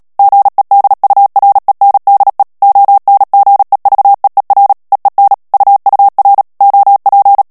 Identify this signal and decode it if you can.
Morse code, heard as a keyed tone, about 37 wpm, 800 Hz. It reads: GEDURENDE ONGEVEER EEN UUR OP
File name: morse mp3 / HOLLANDAIS texte clair 2 vitesse 35 mots minute